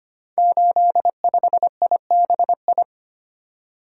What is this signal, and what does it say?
Morse code, heard as a keyed tone, about 25 wpm, 700 Hz. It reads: 85IBI
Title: Code Groups - 5 Letter:Numbers - 25wpm